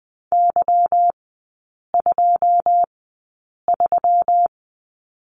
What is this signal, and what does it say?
Morse code, heard as a keyed tone, about 20 wpm, spaced as for 8 wpm, 690 Hz. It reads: Y23
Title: 145 Lesson 24e - Letter y - Three Characters 20wpm